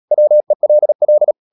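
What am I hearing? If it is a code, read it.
Morse code, heard as a keyed tone, about 37 wpm, 600 Hz. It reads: WELL
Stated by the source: Morse code practice at 35–39 wpm sending English text of Ken Follett's novel Never, from CW med Gnister.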